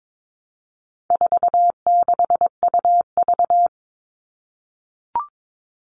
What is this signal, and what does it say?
Morse code, heard as a keyed tone, about 22 wpm, 700 Hz. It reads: 46UV
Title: All Cut Numbers - 4-Digit 22wpm